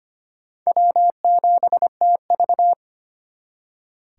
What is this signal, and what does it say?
Morse code, heard as a keyed tone, about 33 wpm, 700 Hz. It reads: W7TV